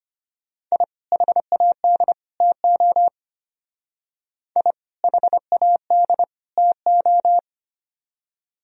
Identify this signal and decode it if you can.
Morse code, heard as a keyed tone, about 32 wpm, 700 Hz. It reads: I HAD TO I HAD TO